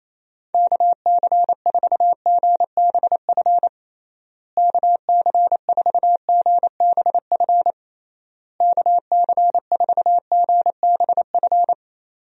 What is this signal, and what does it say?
Morse code, heard as a keyed tone, about 28 wpm, 700 Hz. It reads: KC4GBF KC4GBF KC4GBF
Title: USA Call Signs - 6 Characters - Repeated 3x - 28wpm